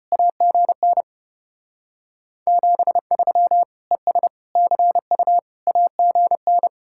Morse code, heard as a keyed tone, about 33 wpm, 700 Hz. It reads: AGN 73 ES CU AGN